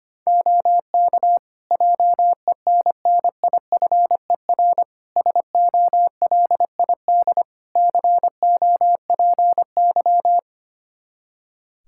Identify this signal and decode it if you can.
Morse code, heard as a keyed tone, about 25 wpm, 700 Hz. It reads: OK JENNIFER SOLID COPY
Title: Practice QSO - Exchange 3 - More Info - 25wpm